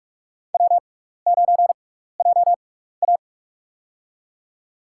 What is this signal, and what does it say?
Morse code, heard as a keyed tone, about 46 wpm, spaced as for 16 wpm, 700 Hz. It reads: W9JA
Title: USA Call Signs - 4 Characters - 6x Character Spacing - 45wpm